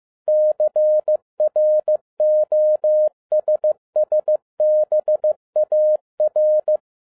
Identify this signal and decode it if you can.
Morse code, heard as a keyed tone, about 15 wpm, 610 Hz. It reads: CROSSBAR
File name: 2012-01-mp3 / n07-ARA JR0DIL-15wpm